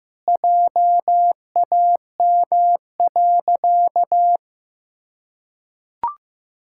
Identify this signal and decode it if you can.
Morse code, heard as a keyed tone, about 15 wpm, 700 Hz. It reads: JAM.
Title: Pangram Challenge v2 - 15wpm